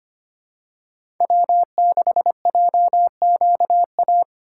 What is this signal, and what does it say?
Morse code, heard as a keyed tone, about 25 wpm, 700 Hz. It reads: W6JQA
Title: USA Call Signs - 5 Characters - 2x Character Spacing - 25wpm